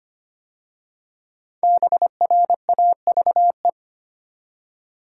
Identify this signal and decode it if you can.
Morse code, heard as a keyed tone, about 25 wpm, 700 Hz. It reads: BRAVE
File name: 5 Letter Words - First 3 Letters Given - 25wpm